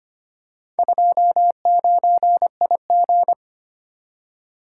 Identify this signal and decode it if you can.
Morse code, heard as a keyed tone, about 25 wpm, 700 Hz. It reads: 29IG
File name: practice / 25wpm / Four Letter-Number Combinations - 25wpm